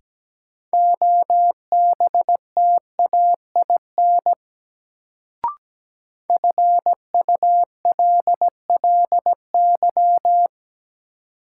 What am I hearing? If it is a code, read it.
Morse code, heard as a keyed tone, about 17 wpm, 700 Hz. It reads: OBTAIN FULLY
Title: Sentences from Top 1300 Words - New Word Drill - 17wpm